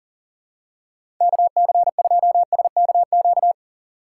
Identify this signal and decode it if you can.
Morse code, heard as a keyed tone, about 40 wpm, 700 Hz. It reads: KC2SKQ
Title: USA Call Signs - 4x Character Spacing - 40wpm